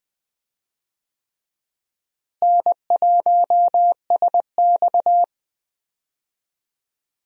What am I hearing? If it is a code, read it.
Morse code, heard as a keyed tone, about 19 wpm, 700 Hz. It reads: N1SX
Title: USA Call Signs - 4 Characters - Repeated 3x - 20wpm